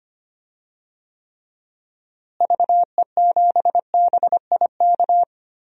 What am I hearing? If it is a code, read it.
Morse code, heard as a keyed tone, about 25 wpm, 700 Hz. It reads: VE7BIK
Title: DXCC-100 Call Signs - 6 Characters - Repeated 3x - 25wpm